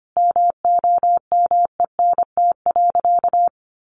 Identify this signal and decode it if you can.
Morse code, heard as a keyed tone, about 25 wpm, 700 Hz. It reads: MOMENT.
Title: Sentences from Top 1000 Words 25wpm